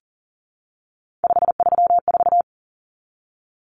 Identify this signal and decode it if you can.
Morse code, heard as a keyed tone, about 40 wpm, 700 Hz. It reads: 534
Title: Common Cut Numbers - 3-Digit 40wpm